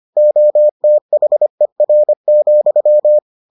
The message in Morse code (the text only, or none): OTHER,